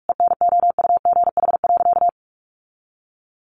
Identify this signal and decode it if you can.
Morse code, heard as a keyed tone, about 44 wpm, 700 Hz. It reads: ENOUGH.
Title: Sentences from Top 900 Words 45wpm